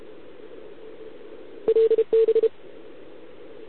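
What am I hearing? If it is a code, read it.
Morse code, heard as a keyed tone, about 30 wpm, 440 Hz. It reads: LB
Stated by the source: Contest CW from OK1RI on 80 m during the OKOM DX contest, recorded by a QRP station using a KX3 and a wire antenna.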